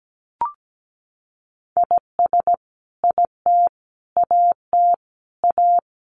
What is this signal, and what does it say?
Morse code, heard as a keyed tone, about 17 wpm, 700 Hz. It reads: IS IT AT A
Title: Sentences from Top 1200 Words 17wpm